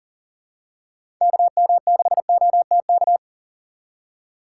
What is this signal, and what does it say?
Morse code, heard as a keyed tone, about 38 wpm, 700 Hz. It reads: KM6OTK
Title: USA Call Signs - 6 Characters - 2x Character Spacing - 40wpm